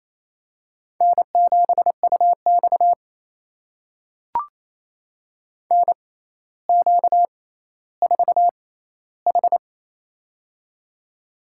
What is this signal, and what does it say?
Morse code, heard as a keyed tone, about 28 wpm, spaced as for 14 wpm, 700 Hz. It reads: N7UX NQ4H
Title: USA Call Signs - 4 Characters - 6x Character Spacing - 28wpm